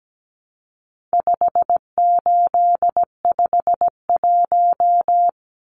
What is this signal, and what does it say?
Morse code, heard as a keyed tone, about 17 wpm, 700 Hz. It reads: 5851
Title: Numbers - 4-Digit 17wpm